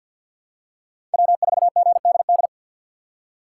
Morse code, heard as a keyed tone, about 54 wpm, 700 Hz. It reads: W4GDD
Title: USA Call Signs - 5 Characters - 8x Character Spacing - 50wpm